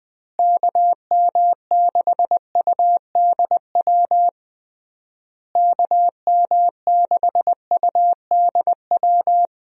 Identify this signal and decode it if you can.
Morse code, heard as a keyed tone, about 21 wpm, 700 Hz. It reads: KM6UDW KM6UDW
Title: USA Call Signs - 6 Characters - Repeated 3x - 20wpm